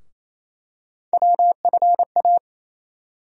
Morse code, heard as a keyed tone, about 27 wpm, 700 Hz. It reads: WFA
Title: USA Call Sign Postfixes - 3 Letters 28wpm